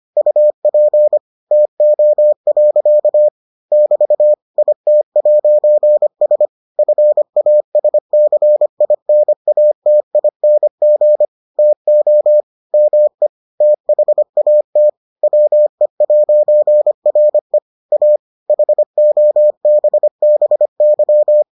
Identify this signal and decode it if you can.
Morse code, heard as a keyed tone, about 25 wpm, 600 Hz. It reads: UP TO. <BT> IT'S FASCINATING TO ME THAT WE'RE A HOBBY